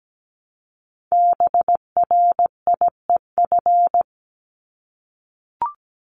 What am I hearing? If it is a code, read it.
Morse code, heard as a keyed tone, about 17 wpm, 700 Hz. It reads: BRIEF
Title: LY Suffix - Top 100 Words with Base Words 17wpm